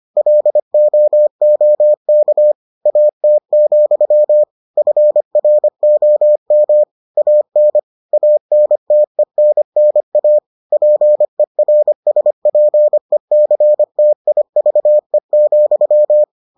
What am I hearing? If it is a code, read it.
Morse code, heard as a keyed tone, about 25 wpm, 600 Hz. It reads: LOOK AT, FROM AN ANTENNA PERSPECTIVE,